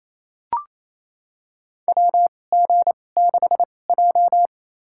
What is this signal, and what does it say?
Morse code, heard as a keyed tone, about 28 wpm, spaced as for 20 wpm, 700 Hz. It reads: WG6J
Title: USA Call Signs - 4 Characters - 2x Character Spacing - 28wpm